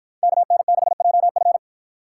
Morse code, heard as a keyed tone, about 51 wpm, 700 Hz. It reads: KN6JF